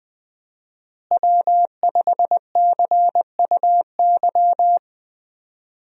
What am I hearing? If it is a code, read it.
Morse code, heard as a keyed tone, about 20 wpm, 700 Hz. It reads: W5CUY
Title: USA Call Signs - 5 Characters - Repeated 2x - 20wpm